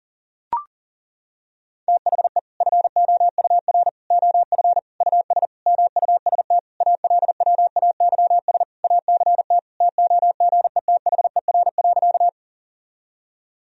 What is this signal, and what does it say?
Morse code, heard as a keyed tone, about 40 wpm, 700 Hz. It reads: THE FOUR OF US MUST ALWAYS ACT TOGETHER.
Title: Sentences from Top 1000 Words - Review Part 1 - 40wpm